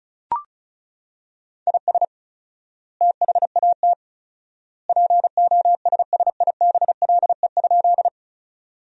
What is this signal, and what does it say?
Morse code, heard as a keyed tone, about 36 wpm, 700 Hz. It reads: IS THAT POSSIBLE?
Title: 4x Word-Spacing - Sentences from Top 500 Words - 35wpm